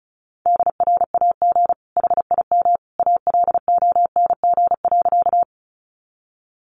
Morse code, heard as a keyed tone, about 34 wpm, 700 Hz. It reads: DRAG HIM ALONG.
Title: Sentences from Top 1500 Words 35wpm